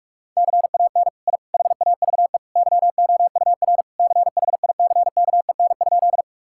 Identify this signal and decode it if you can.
Morse code, heard as a keyed tone, about 42 wpm, 700 Hz. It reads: CAN I HAVE YOUR CHICKEN?
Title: Sentences from Top 1000 Words - Review Part 5 - 45wpm